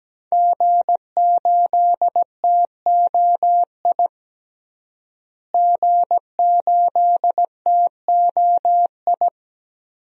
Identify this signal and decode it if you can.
Morse code, heard as a keyed tone, about 17 wpm, 700 Hz. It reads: G8TOI G8TOI
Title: DXCC-100 Call Signs - 5 Characters - Repeated 3x - 17wpm